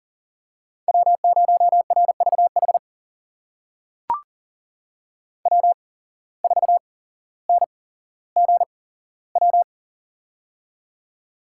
Morse code, heard as a keyed tone, about 40 wpm, spaced as for 17 wpm, 700 Hz. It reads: W0RVH W4NGW
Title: USA Call Signs - 5 Characters - 8x Character Spacing - 40wpm